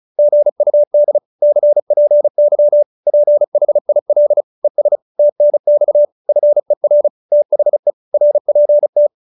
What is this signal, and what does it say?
Morse code, heard as a keyed tone, about 34 wpm, 590 Hz. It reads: GUD CPY PHIL ES TNX FER THE RPT